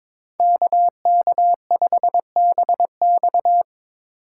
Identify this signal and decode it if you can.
Morse code, heard as a keyed tone, about 26 wpm, 700 Hz. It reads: KK5BX
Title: Speed Racing - USA Call Signs 22wpm